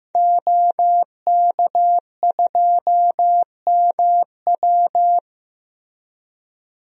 Morse code, heard as a keyed tone, about 15 wpm, 700 Hz. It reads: OK2MW
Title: DXCC-100 Call Signs - 5 Characters - Repeated 3x - 15wpm